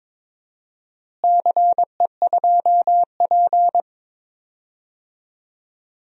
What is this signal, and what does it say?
Morse code, heard as a keyed tone, about 22 wpm, 700 Hz. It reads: CE2P
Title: DXCC-100 Call Signs - 4 Characters - Repeated 3x - 22wpm